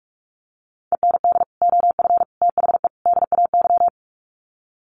Morse code, heard as a keyed tone, about 44 wpm, 700 Hz. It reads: END OF THE DAY
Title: Sets of 4 Words 45wpm